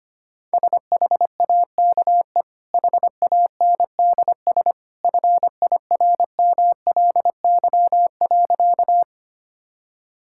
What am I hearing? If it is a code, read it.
Morse code, heard as a keyed tone, about 25 wpm, 700 Hz. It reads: SHAKE HANDS FIRMLY.
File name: Easy Sentences with Hints - 25wpm